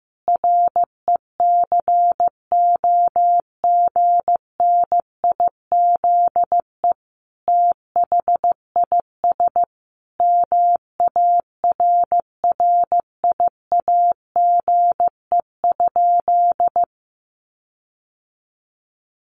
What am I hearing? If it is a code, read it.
Morse code, heard as a keyed tone, about 15 wpm, 700 Hz. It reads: RECOGNIZE THIS MARRIAGE?
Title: Sentences from Top 1400 Words 15wpm